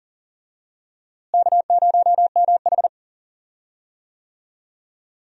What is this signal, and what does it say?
Morse code, heard as a keyed tone, about 40 wpm, 700 Hz. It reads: K0MH
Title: USA Call Signs - 4 Characters - Repeated 3x - 40wpm